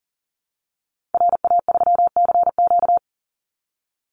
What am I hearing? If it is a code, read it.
Morse code, heard as a keyed tone, about 40 wpm, 700 Hz. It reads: RA3CQ
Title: Call Signs Repeated 3x - 40wpm